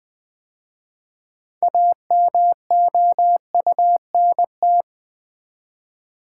Morse code, heard as a keyed tone, about 20 wpm, 700 Hz. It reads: AMOUNT
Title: 6 Letter Words - First 3 Letters Given - 20wpm